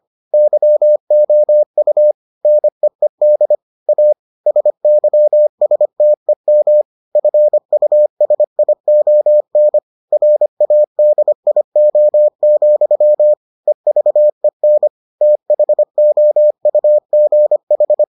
Morse code, heard as a keyed tone, about 25 wpm, 600 Hz. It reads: YOU NEED A SYSTEM FUSION RADIO, EVEN THOUGH